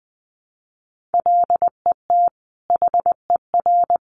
Morse code, heard as a keyed tone, about 20 wpm, 700 Hz. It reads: LET HER
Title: Sentences from Top 1200 Words 20wpm